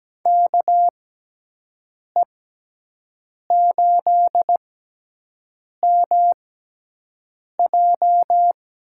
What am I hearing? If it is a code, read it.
Morse code, heard as a keyed tone, about 17 wpm, spaced as for 6 wpm, 700 Hz. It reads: KE8MJ
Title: USA Call Signs - 6 Characters - 6x Character Spacing - 17wpm